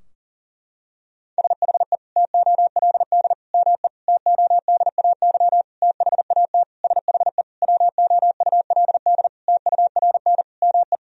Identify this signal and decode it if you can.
Morse code, heard as a keyed tone, about 40 wpm, 700 Hz. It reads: SHE TOLD ME TODAY THAT SHE WOULD TURN ME
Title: Sentences from Top 1000 Words - Review Part 4 - 40wpm